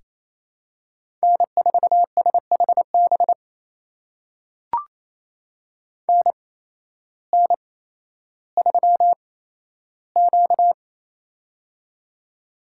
Morse code, heard as a keyed tone, about 28 wpm, spaced as for 13 wpm, 700 Hz. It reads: N4SHB NN3Q
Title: USA Call Signs - 8x Character Spacing - 28wpm